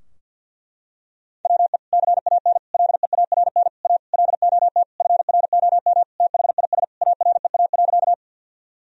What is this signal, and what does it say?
Morse code, heard as a keyed tone, about 52 wpm, 700 Hz. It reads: WE CAN LEARN A LOT FROM THIS AREA.
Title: Sentences from Top 1000 Words - Review Part 2 - 50wpm